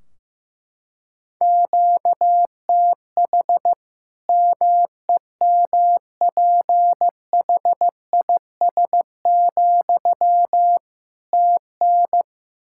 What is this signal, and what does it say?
Morse code, heard as a keyed tone, about 15 wpm, 700 Hz. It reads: QTH MEMPHIS, TN
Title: Practice QSO - Exchange 1 - Beginning - 15wpm